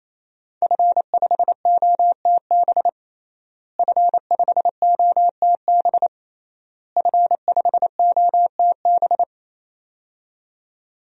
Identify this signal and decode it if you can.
Morse code, heard as a keyed tone, about 28 wpm, 700 Hz. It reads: F5OTB F5OTB F5OTB
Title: DXCC-100 Call Signs - 5 Characters - Repeated 3x - 28wpm